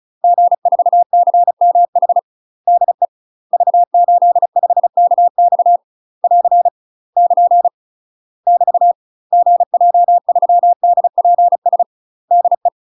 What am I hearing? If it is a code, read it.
Morse code, heard as a keyed tone, about 34 wpm, 700 Hz. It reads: G4CMH DE V85KX <AR> <KN> <BT> GJ3DPS DE
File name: UK MASTER QSOs TRAINING/35WPM